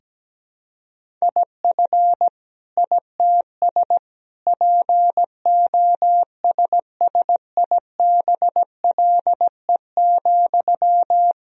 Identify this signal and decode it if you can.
Morse code, heard as a keyed tone, about 17 wpm, 700 Hz. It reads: IF ITS POSSIBLE,
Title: Sentences from Top 1000 Words - Review Part 2 - 17wpm